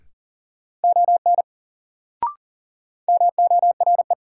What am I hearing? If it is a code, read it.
Morse code, heard as a keyed tone, about 37 wpm, 700 Hz. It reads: ON MORE
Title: Sentences from Top 100 Words - New Word Drill - 40wpm